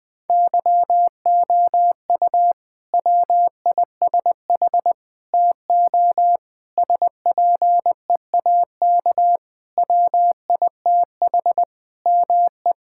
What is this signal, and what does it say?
Morse code, heard as a keyed tone, about 21 wpm, 700 Hz. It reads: YOU WISH TO SPEAK WITH ME
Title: Sentences from Top 1600 Words 20wpm